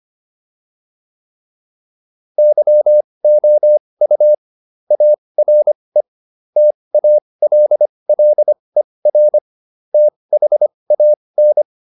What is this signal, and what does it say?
Morse code, heard as a keyed tone, about 25 wpm, spaced as for 20 wpm, 600 Hz. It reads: YOU ARE TALLER THAN
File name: SS123 20